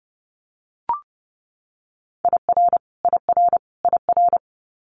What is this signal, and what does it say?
Morse code, heard as a keyed tone, about 30 wpm, 700 Hz. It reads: IR IR IR
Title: Multi-letter Phonemes Repeated 3x 30wpm